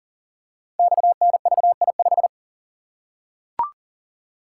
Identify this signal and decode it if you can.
Morse code, heard as a keyed tone, about 39 wpm, 700 Hz. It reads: XNVI5
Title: Code Groups - 5 Letter:Numbers - 40wpm